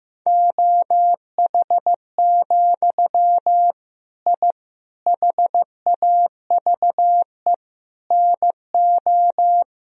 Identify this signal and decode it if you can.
Morse code, heard as a keyed tone, about 15 wpm, 700 Hz. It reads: OH, I HAVE NO
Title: Sentences from Top 1000 Words - Review Part 3 - 15wpm